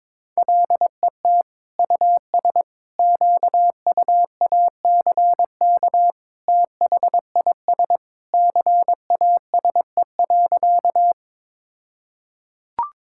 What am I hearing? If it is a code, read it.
Morse code, heard as a keyed tone, about 22 wpm, 700 Hz. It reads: LET US QUACK THIS CASE.
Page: morse-code-ninja-practice-22wpm